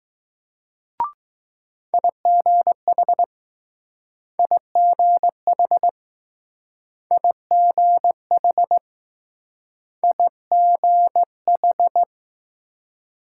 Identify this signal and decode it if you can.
Morse code, heard as a keyed tone, about 20 wpm, 700 Hz. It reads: IGH IGH IGH IGH